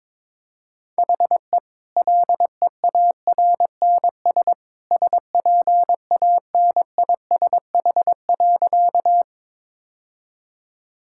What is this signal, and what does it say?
Morse code, heard as a keyed tone, about 22 wpm, 700 Hz. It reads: HE LEARNS SPANISH.
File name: practice / 22wpm / Easy Sentences with Hints - 22wpm